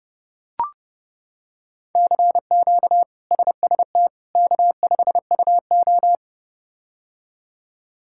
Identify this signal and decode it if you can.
Morse code, heard as a keyed tone, about 32 wpm, 700 Hz. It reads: CQ SST K5UO
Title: K1USN SST QSO Practice - 30wpm